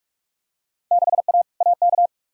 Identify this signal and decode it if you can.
Morse code, heard as a keyed tone, about 42 wpm, 700 Hz. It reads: 6A AK
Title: Field Day Exchanges Encore - 45wpm